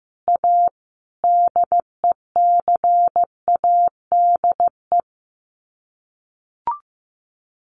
Morse code, heard as a keyed tone, about 15 wpm, 700 Hz. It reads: A DECADE
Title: Sets of 5 Words 15wpm